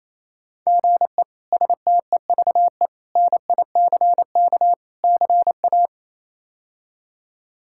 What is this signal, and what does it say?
Morse code, heard as a keyed tone, about 28 wpm, 700 Hz. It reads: GE STEVE NICK CA
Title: K1USN SST QSO Practice - 28wpm